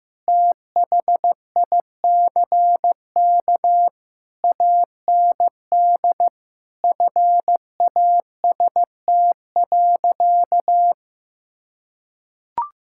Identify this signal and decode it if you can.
Morse code, heard as a keyed tone, about 15 wpm, 700 Hz. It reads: THICK AND FAST.
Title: Sentences from Top 1600 Words 15wpm